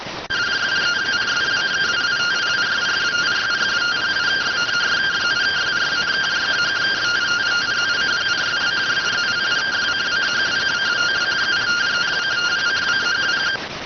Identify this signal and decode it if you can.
Two-tone teleprinter signal, about 170 Hz shift.